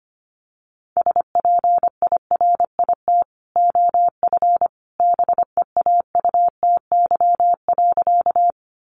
Morse code, heard as a keyed tone, about 25 wpm, 700 Hz. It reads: SPIRIT OF BEAUTY.
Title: Sentences from Top 1400 Words 25wpm